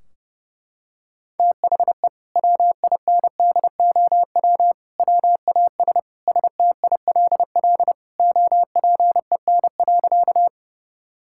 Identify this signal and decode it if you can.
Morse code, heard as a keyed tone, about 30 wpm, 700 Hz. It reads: THE WINDOW WAS STILL OPEN.
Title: Sentences from Top 1600 Words 30wpm